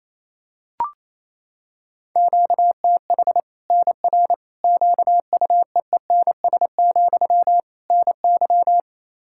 Morse code, heard as a keyed tone, about 28 wpm, 700 Hz. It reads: QTH NR QUEENS, NY
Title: Practice QSO - Exchange 1 - Beginning - 28wpm